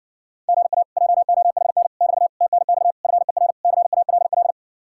Morse code, heard as a keyed tone, about 60 wpm, 700 Hz. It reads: GA JOHN <BT> TNX FER CALL